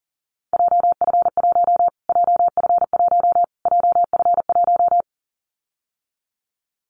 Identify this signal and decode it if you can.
Morse code, heard as a keyed tone, about 40 wpm, 700 Hz. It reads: JF1 JF1 JF1
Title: Callsign Prefixes for Top 25 DXCC - Repeated 3x - 40wpm